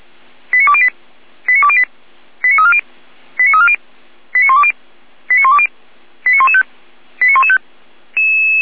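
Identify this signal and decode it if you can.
Five-tone selective call (ZVEI), heard as quick runs of five tones, tones of about 70 ms. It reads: 89208 89208 89340 89340 89120 89120 89115 89115
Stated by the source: ZVEI five-tone call received on SDR.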